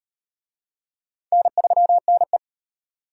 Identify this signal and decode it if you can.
Morse code, heard as a keyed tone, about 38 wpm, 700 Hz. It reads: N3NE